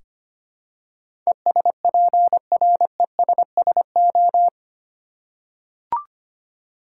Morse code, heard as a keyed tone, about 25 wpm, 700 Hz. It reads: ESPRESSO
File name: practice / 25wpm / Difficult Words with Hints - 25wpm